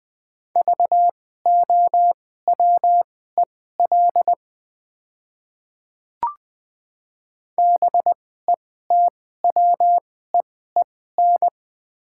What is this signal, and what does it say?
Morse code, heard as a keyed tone, about 20 wpm, spaced as for 14 wpm, 700 Hz. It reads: VOWEL BETWEEN
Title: 2x Character Spacing - Top 500 Words - 20wpm